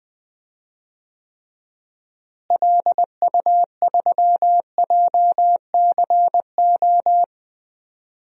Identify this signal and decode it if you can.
Morse code, heard as a keyed tone, about 20 wpm, 700 Hz. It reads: LU3JCO
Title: DXCC-100 Call Signs - 6 Characters - Repeated 3x - 20wpm